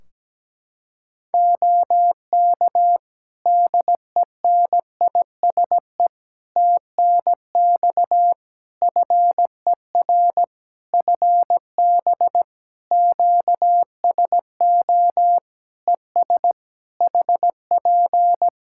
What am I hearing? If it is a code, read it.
Morse code, heard as a keyed tone, about 17 wpm, 700 Hz. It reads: OK DENISE TNX FER FB QSO ES HP
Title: K1USN SST QSO Practice 17wpm